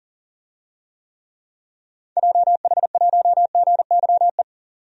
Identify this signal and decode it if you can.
Morse code, heard as a keyed tone, about 40 wpm, 700 Hz. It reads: JH1GYE
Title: DXCC-100 Call Signs - 6 Characters - 40wpm